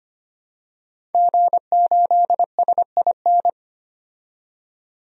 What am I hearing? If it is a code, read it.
Morse code, heard as a keyed tone, about 25 wpm, 700 Hz. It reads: G8SIN
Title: DXCC-100 Call Signs - 5 Characters - 25wpm